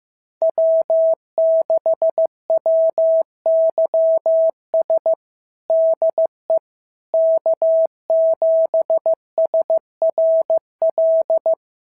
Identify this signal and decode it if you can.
Morse code, heard as a keyed tone, about 14 wpm, 650 Hz. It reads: W6WYS DE K7SRL
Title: SOTA QSO Practice 15wpm